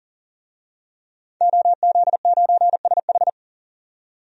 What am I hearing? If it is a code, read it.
Morse code, heard as a keyed tone, about 40 wpm, 700 Hz. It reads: OZ9SH
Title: DXCC-100 Call Signs - 5 Characters - 40wpm